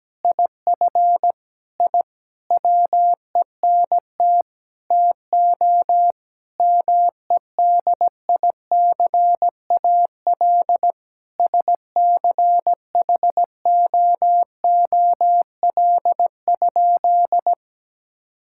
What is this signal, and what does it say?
Morse code, heard as a keyed tone, about 17 wpm, 700 Hz. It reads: IF I WENT TO MEDICAL SCHOOL?
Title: Sentences from Top 1000 Words - Review Part 5 - 17wpm